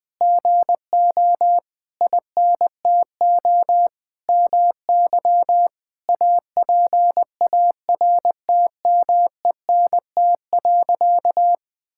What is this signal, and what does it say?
Morse code, heard as a keyed tone, about 20 wpm, 700 Hz. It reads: GO INTO MY APARTMENT.
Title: Sentences from Top 1000 Words - Review Part 6 - 20wpm